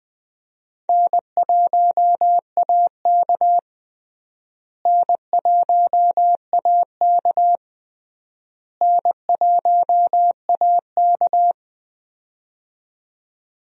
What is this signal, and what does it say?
Morse code, heard as a keyed tone, about 21 wpm, 700 Hz. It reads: N1AK N1AK N1AK
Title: USA Call Signs - 4 Characters - Repeated 3x - 20wpm